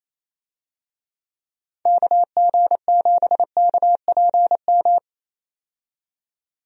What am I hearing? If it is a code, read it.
Morse code, heard as a keyed tone, about 27 wpm, 700 Hz. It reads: KG7KPM
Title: USA Call Signs - 6 Characters - 28wpm